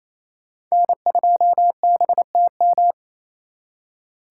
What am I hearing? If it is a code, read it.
Morse code, heard as a keyed tone, about 34 wpm, 700 Hz. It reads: N2BTM
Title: Speed Racing - USA Call Signs 28wpm